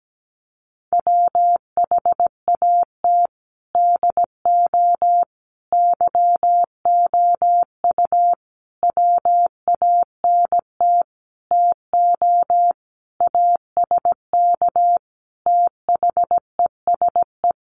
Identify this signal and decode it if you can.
Morse code, heard as a keyed tone, about 17 wpm, 700 Hz. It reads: WHAT DO YOU WANT TO ASK THESE
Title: Sentences from Top 200 Words 17wpm